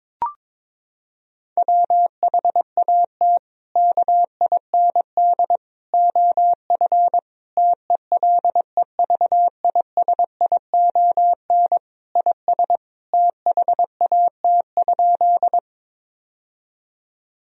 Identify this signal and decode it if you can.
Morse code, heard as a keyed tone, about 21 wpm, 700 Hz. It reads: WHAT KIND OF TELEVISION IS THAT?
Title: Sentences from Top 1000 Words - Review Part 4 - 22wpm